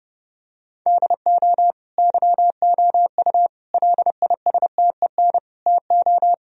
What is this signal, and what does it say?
Morse code, heard as a keyed tone, about 30 wpm, 700 Hz. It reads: DO YOU LISTEN TO
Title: Sentences from Top 1000 Words - Review Part 3 - 30wpm